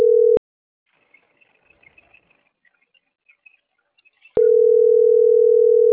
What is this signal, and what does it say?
Ringback tone (North America), 440 + 480 Hz.